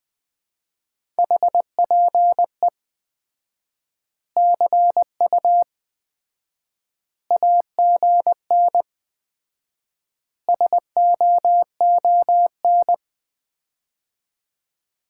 Morse code, heard as a keyed tone, about 20 wpm, 700 Hz. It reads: HPE CU AGN SOON
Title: Common QSO Phrases - 4x Word-Spacing - 20wpm